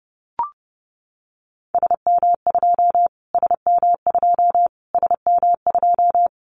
Morse code, heard as a keyed tone, about 30 wpm, 700 Hz. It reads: SM2 SM2 SM2
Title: Callsign Prefixes for Top 25 DXCC - Repeated 3x - 30wpm